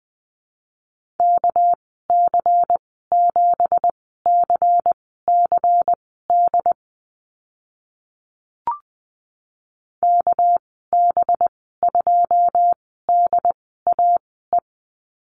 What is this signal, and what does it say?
Morse code, heard as a keyed tone, about 20 wpm, spaced as for 15 wpm, 700 Hz. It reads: KC7CCD KB2DAE
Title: 200 Lesson 32g - Letter k - Call Signs 20wpm